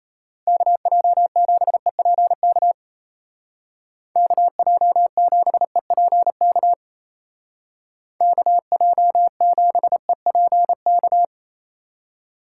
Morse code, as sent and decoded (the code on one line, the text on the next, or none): -.- .--- --... .--. -.- / -.- .--- --... .--. -.- / -.- .--- --... .--. -.-
KJ7EPK KJ7EPK KJ7EPK